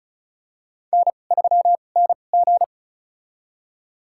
Morse code, heard as a keyed tone, about 36 wpm, spaced as for 25 wpm, 700 Hz. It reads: N3NG
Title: USA Call Signs - 4 Characters - 2x Character Spacing - 35wpm